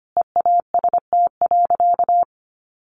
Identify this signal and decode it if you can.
Morse code, heard as a keyed tone, about 25 wpm, 700 Hz. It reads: EAST.
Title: Sentences from Top 1300 Words 25wpm